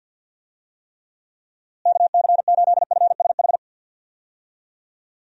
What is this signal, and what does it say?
Morse code, heard as a keyed tone, about 52 wpm, 700 Hz. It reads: KC8FSH
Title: USA Call Signs - 6 Characters - Repeated 2x - 50wpm